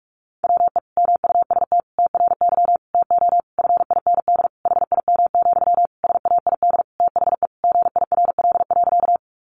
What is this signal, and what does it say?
Morse code, heard as a keyed tone, about 44 wpm, 700 Hz. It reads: WE MUST TRY TO FIND HIM, SAID THE GIRL.
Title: Sentences from Top 400 Words 45wpm